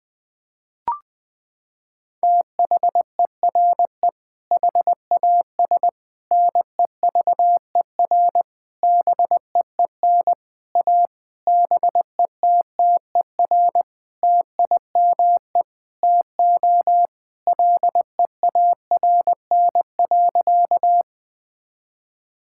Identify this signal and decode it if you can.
Morse code, heard as a keyed tone, about 21 wpm, 700 Hz. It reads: THERE HAS NEVER BEEN A BETTER TIME TO LEARN.
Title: Sentences with 8-Word Phrases - 20wpm